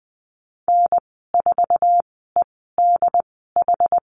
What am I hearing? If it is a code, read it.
Morse code, heard as a keyed tone, about 20 wpm, spaced as for 15 wpm, 700 Hz. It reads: N4EDH
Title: 088 Lesson 15g - Number 5 - Call Signs 20wpm